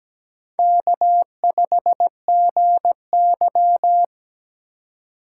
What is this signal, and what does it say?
Morse code, heard as a keyed tone, about 20 wpm, 700 Hz. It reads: K5GY
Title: Speed Racing - USA 1x2 Call Signs 17wpm